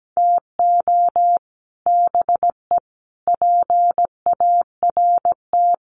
Morse code, heard as a keyed tone, about 17 wpm, 700 Hz. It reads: TO BE PART
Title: Sets of 5 Words Encore2 17wpm